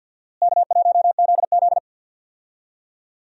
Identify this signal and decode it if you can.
Morse code, heard as a keyed tone, about 51 wpm, 700 Hz. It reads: K1ZZ
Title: USA Call Signs - 4 Characters - Repeated 3x - 50wpm